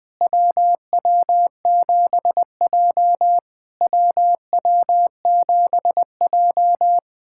Morse code, heard as a keyed tone, about 20 wpm, 700 Hz. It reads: WW7J WW7J
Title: SOTA QSO Practice 20wpm